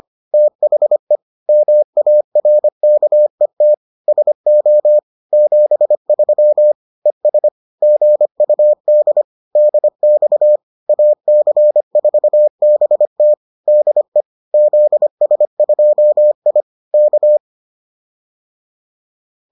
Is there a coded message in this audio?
Yes: Morse code from 0.3 to 17.4 s, heard as a keyed tone, about 25 wpm, 600 Hz. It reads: THE MARKET SO 73 ES GUD DX AC4BT DE ZS2I K